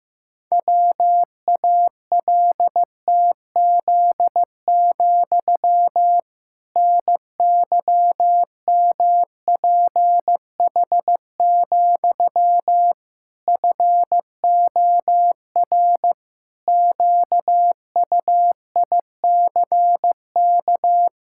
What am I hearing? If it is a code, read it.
Morse code, heard as a keyed tone, about 15 wpm, 700 Hz. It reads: WALTZ, NYMPH, FOR QUICK
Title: Pangram Challenge v2 - 15wpm